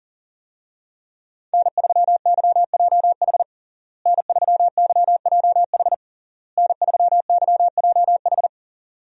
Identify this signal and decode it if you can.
Morse code, heard as a keyed tone, about 40 wpm, 700 Hz. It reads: N3YJH N3YJH N3YJH
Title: USA Call Signs - 5 Characters - Repeated 3x - 40wpm